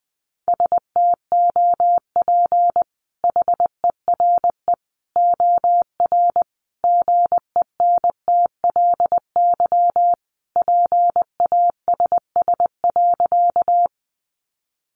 Morse code, heard as a keyed tone, about 20 wpm, 700 Hz. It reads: STOP HERE OR GENTLY PASS.
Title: Sentences from Top 1500 Words 20wpm